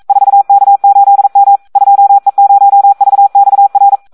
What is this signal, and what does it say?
Morse code, heard as a keyed tone, about 42 wpm, 800 Hz. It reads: 4K8M 2E0VXR
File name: Audio practica CW / SD 1 42